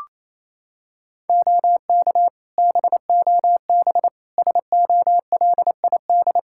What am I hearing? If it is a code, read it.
Morse code, heard as a keyed tone, about 28 wpm, 700 Hz. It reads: OK BOB SOLID
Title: Practice QSO - Exchange 3 - More Info - 28wpm